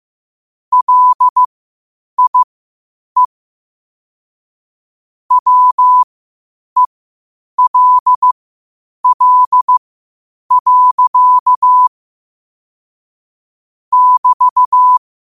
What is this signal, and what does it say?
Morse code, heard as a keyed tone, about 15 wpm, spaced as for 8 wpm, 1000 Hz. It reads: LIE WELL. <BT>
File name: qotdfiles / 8wpm